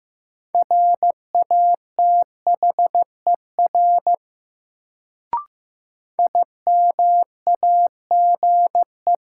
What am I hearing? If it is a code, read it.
Morse code, heard as a keyed tone, about 15 wpm, 700 Hz. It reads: RATHER IMAGE
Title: Sentences from Top 700 Words - New Word Drill - 15wpm